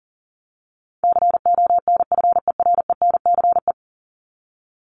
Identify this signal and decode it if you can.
Morse code, heard as a keyed tone, about 40 wpm, 700 Hz. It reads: CONFERENCE
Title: Top 1751-2000 Words 40wpm